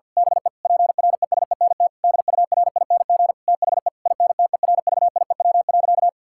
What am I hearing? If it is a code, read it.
Morse code, heard as a keyed tone, about 52 wpm, 700 Hz. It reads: BE PRESENT DURING THE INTERVIEW.
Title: Sentences from Top 1600 Words 50wpm